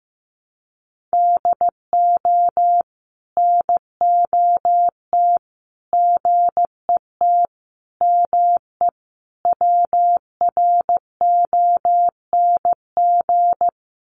Morse code, heard as a keyed tone, about 15 wpm, 700 Hz. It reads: DO NOT GET ME WRONG
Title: Sets of 5 Words Encore 15wpm